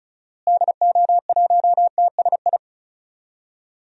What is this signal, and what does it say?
Morse code, heard as a keyed tone, about 35 wpm, 700 Hz. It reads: DO1TSI
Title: DXCC-100 Call Signs - 6 Characters - Repeated 3x - 35wpm